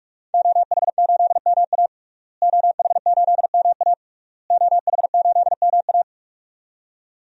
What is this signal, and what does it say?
Morse code, heard as a keyed tone, about 46 wpm, 700 Hz. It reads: OH8MA OH8MA OH8MA
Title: DXCC-100 Call Signs - 5 Characters - Repeated 3x - 45wpm